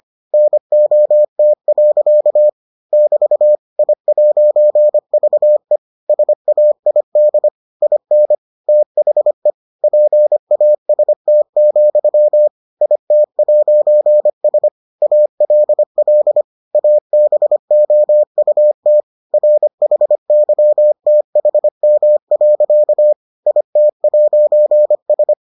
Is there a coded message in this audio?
Yes: Morse code from 0.3 to 25.3 s, heard as a keyed tone, about 25 wpm, 600 Hz. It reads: NOT. <BT> I'VE SAID IN THE PAST, IT'S ALL ABOUT RHYTHM. IT'S